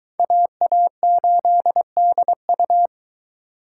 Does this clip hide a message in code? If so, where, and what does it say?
Yes: Morse code from 0.2 to 2.9 s, heard as a keyed tone, about 26 wpm, 700 Hz. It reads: AA8DU